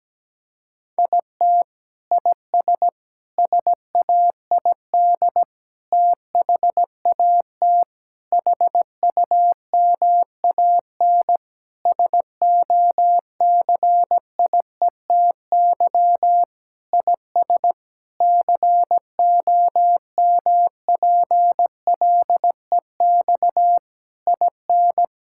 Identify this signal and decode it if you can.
Morse code, heard as a keyed tone, about 17 wpm, 700 Hz. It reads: IT IS SAID THAT HUMAN SOCIETY IS COMPLEX IN
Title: Sentences from Top 1000 Words - Review Part 6 - 17wpm